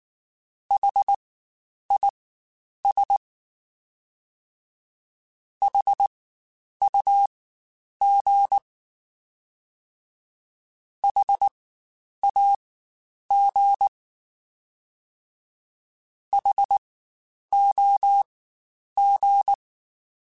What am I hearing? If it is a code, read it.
Morse code, heard as a keyed tone, about 19 wpm, spaced as for 9 wpm, 800 Hz. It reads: HIS HUG HAG HOG
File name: Spaced Alphas / Spaced Alphas mp3s / Hs sp @18